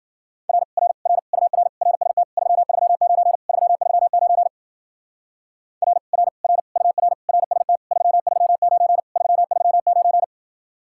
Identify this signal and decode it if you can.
Morse code, heard as a keyed tone, about 49 wpm, 700 Hz. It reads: R R R UR RST 339 339 R R R UR RST 339 339
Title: Speed-Racing Common QSO Phrases Encore - 40wpm